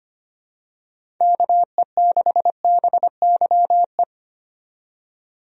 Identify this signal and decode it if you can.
Morse code, heard as a keyed tone, about 25 wpm, 700 Hz. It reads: KE6BYE
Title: USA Call Signs - 6 Characters - 25wpm